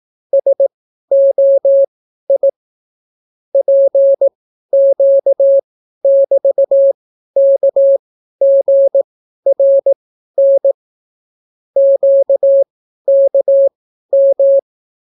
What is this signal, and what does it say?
Morse code, heard as a keyed tone, about 18 wpm, spaced as for 12 wpm, 550 Hz. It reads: SOI PQ<BT>KGRN QKM